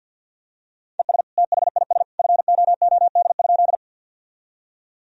Morse code, heard as a keyed tone, about 52 wpm, 700 Hz. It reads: ES THIS FOOD?
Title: Sentences from Top 300 Words and Abbreviations 50wpm